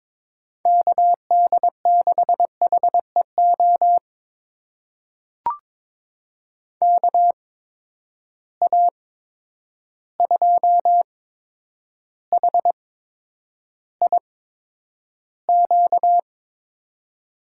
Morse code, heard as a keyed tone, about 21 wpm, spaced as for 9 wpm, 700 Hz. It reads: KD6HEO KA2HIQ
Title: USA Call Signs - 6 Characters - 8x Character Spacing - 22wpm